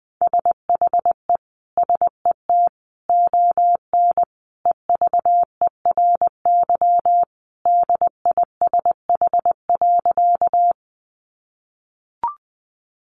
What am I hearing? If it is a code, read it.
Morse code, heard as a keyed tone, about 20 wpm, 700 Hz. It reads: SHE SET ON EVERY DISH.
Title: Sentences from Top 1300 Words 20wpm